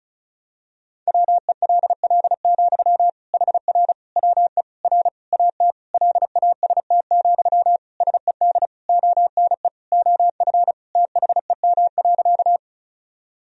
Morse code, heard as a keyed tone, about 36 wpm, 700 Hz. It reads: WELL, HR WE R AT LAST, SED ONE OF THEM.
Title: Sentences from Top 300 Words and Abbreviations 35wpm